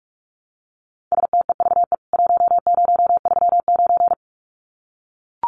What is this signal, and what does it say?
Morse code, heard as a keyed tone, about 44 wpm, 700 Hz. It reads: STEVE 1039
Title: CWT Exchange 45wpm